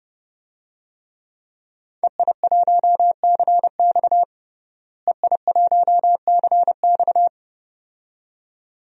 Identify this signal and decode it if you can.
Morse code, heard as a keyed tone, about 30 wpm, 700 Hz. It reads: EI1CX EI1CX
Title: DXCC-100 Call Signs - 5 Characters - Repeated 2x - 30wpm